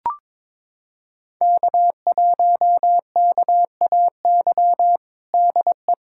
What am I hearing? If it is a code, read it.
Morse code, heard as a keyed tone, about 22 wpm, 700 Hz. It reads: K1KAY DE
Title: USA Callsign DE Callsign - 5 Characters - 22wpm